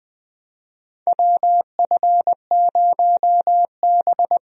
Morse code, heard as a keyed tone, about 21 wpm, 700 Hz. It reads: WF0B
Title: USA Call Signs - 4 Characters - Repeated 3x - 20wpm